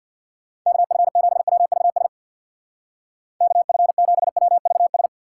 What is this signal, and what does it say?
Morse code, heard as a keyed tone, about 61 wpm, 700 Hz. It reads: KF7WVS KF7WVS
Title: Speed Racing - USA 2x3 Call Signs 50wpm